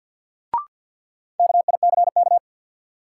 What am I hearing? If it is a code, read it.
Morse code, heard as a keyed tone, about 52 wpm, 700 Hz. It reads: KICK
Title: Sentences from Top 1300 Words - New Word Drill Alt - 50wpm